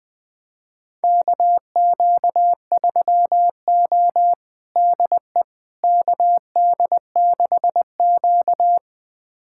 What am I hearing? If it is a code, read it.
Morse code, heard as a keyed tone, about 20 wpm, 700 Hz. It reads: KQ3O DE KD6Q
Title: USA Callsign DE Callsign - 4 Characters - 20wpm